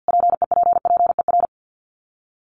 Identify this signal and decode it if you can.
Morse code, heard as a keyed tone, about 50 wpm, 700 Hz. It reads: PEPPER
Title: Sets of 5 Words Encore 50wpm